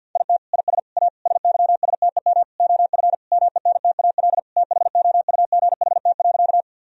Morse code, heard as a keyed tone, about 50 wpm, 700 Hz. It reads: IT IS A SYSTEM OF MENTAL THOUGHT.
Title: Sentences from Top 1000 Words - Review Part 4 - 50wpm